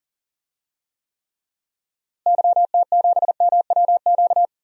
Morse code, heard as a keyed tone, about 40 wpm, 700 Hz. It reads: YT7MWQ
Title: DXCC-100 Call Signs - 6 Characters - Repeated 3x - 40wpm